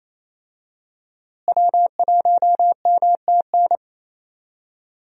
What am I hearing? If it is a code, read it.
Morse code, heard as a keyed tone, about 28 wpm, 700 Hz. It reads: W1MTN
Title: USA Call Signs - 5 Characters - Repeated 3x - 28wpm